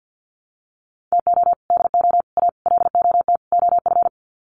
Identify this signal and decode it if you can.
Morse code, heard as a keyed tone, about 50 wpm, 700 Hz. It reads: TO DO A LOT OF
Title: Sets of 5 Words Encore 50wpm